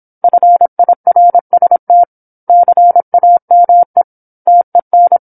Morse code, heard as a keyed tone, about 27 wpm, 700 Hz. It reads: FIRST CAME TEN
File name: 08-Chapter-08 26 WPM